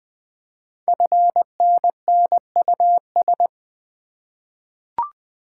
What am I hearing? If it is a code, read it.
Morse code, heard as a keyed tone, about 20 wpm, 700 Hz. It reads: FNNUS
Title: Code Groups - 5 Letters - 20wpm